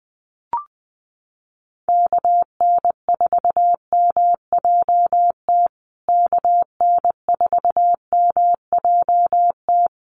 Morse code, heard as a keyed tone, about 20 wpm, 700 Hz. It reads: KN4MJT KN4MJT
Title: USA Call Signs Repeated 3x 20wpm